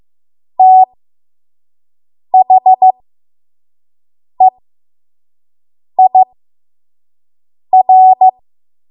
Morse code, heard as a keyed tone, about 15 wpm, spaced as for 5 wpm, 750 Hz. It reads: THEIR
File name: Morse / 05WPM